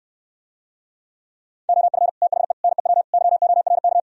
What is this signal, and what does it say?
Morse code, heard as a keyed tone, about 68 wpm, 700 Hz. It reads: OF THE NEW YORK